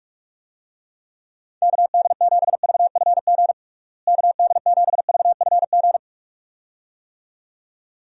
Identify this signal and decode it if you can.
Morse code, heard as a keyed tone, about 46 wpm, 700 Hz. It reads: KD7VFG KD7VFG
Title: USA Call Signs - 6 Characters - Repeated 2x - 45wpm